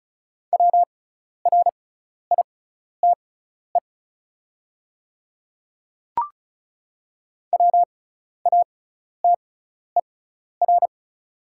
Morse code, heard as a keyed tone, about 36 wpm, spaced as for 12 wpm, 700 Hz. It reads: WRITE WATER
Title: 6x Character Spacing - Top 100 Words - 35wpm